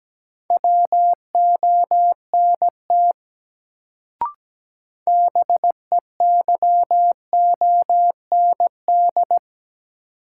Morse code, heard as a keyed tone, about 17 wpm, 700 Hz. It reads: WONT BEYOND
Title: Sentences from Top 1000 Words - Review Part 6 - New Word Drill - 17wpm